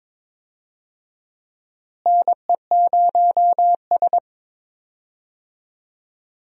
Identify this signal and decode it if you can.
Morse code, heard as a keyed tone, about 21 wpm, 700 Hz. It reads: NE0S
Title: USA Call Signs - 4 Characters - Repeated 2x - 22wpm